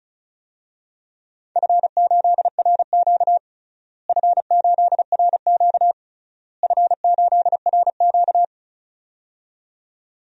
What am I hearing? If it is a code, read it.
Morse code, heard as a keyed tone, about 34 wpm, 700 Hz. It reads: F8RQ F8RQ F8RQ
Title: DXCC-100 Call Signs - 4 Characters - Repeated 3x - 35wpm